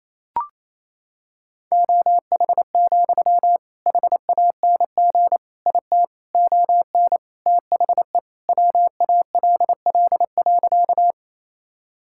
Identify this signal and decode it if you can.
Morse code, heard as a keyed tone, about 28 wpm, 700 Hz. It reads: OH, HANG IT ON THE WALL.
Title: Sentences from Top 1600 Words 28wpm